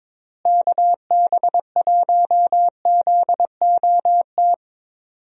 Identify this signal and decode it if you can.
Morse code, heard as a keyed tone, about 22 wpm, 700 Hz. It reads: KB1ZOT
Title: USA Call Signs - 6 Characters - Repeated 3x - 22wpm